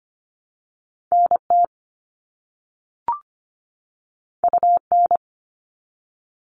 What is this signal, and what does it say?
Morse code, heard as a keyed tone, about 25 wpm, 700 Hz. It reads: NT UN
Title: All Cut Numbers - 2-Digit 25wpm